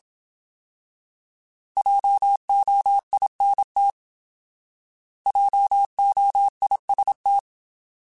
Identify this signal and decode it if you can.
Morse code, heard as a keyed tone, about 27 wpm, 800 Hz. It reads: JOINT JOIST
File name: Alphabetical / Alphabetical mp3s / Js @25